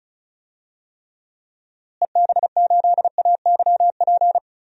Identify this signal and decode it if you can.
Morse code, heard as a keyed tone, about 34 wpm, 700 Hz. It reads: EB8AYP
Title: DXCC-100 Call Signs - 6 Characters - Repeated 3x - 35wpm